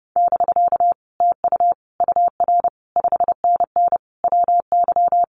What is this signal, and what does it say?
Morse code, heard as a keyed tone, about 30 wpm, 700 Hz. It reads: <BK> TU UR 5NN WY
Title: Common QSO Phrases Encore 30wpm